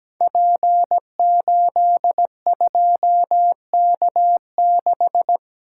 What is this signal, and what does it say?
Morse code, heard as a keyed tone, about 17 wpm, 700 Hz. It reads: P82K6
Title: Code Groups - 5 Letter:Numbers - 17wpm